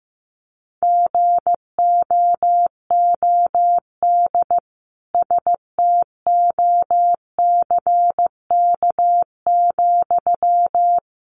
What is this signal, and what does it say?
Morse code, heard as a keyed tone, about 15 wpm, 700 Hz. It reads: GOOD STOCK,
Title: Sentences from Top 800 Words 15wpm